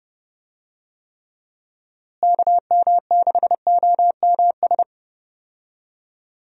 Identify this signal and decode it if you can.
Morse code, heard as a keyed tone, about 30 wpm, 700 Hz. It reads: KM6OMS
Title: USA Call Signs - 6 Characters - Repeated 2x - 30wpm